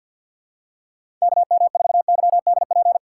Morse code, heard as a keyed tone, about 47 wpm, 700 Hz. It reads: KM4YDP
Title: USA Call Signs - 4x Character Spacing - 50wpm